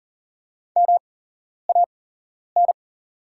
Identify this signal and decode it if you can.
Morse code, heard as a keyed tone, about 40 wpm, spaced as for 11 wpm, 690 Hz. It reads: MAN
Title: 8x Character Spacing - Top 300 Words - 40wpm